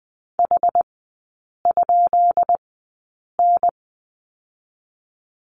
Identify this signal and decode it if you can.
Morse code, heard as a keyed tone, about 20 wpm, spaced as for 8 wpm, 690 Hz. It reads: H?N
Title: 133 Lesson 22e - Question Mark - Three Characters 20wpm